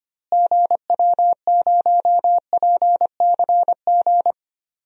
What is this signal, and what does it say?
Morse code, heard as a keyed tone, about 25 wpm, 700 Hz. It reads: GW0PCG
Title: DXCC-100 Call Signs - 6 Characters - Repeated 3x - 25wpm